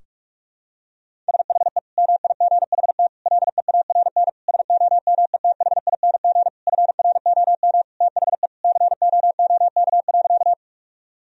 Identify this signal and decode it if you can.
Morse code, heard as a keyed tone, about 41 wpm, 700 Hz. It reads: SHE MIGHT LEARN SOMETHING FROM THE COOK.
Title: Sentences from Top 1000 Words - Review Part 4 - 45wpm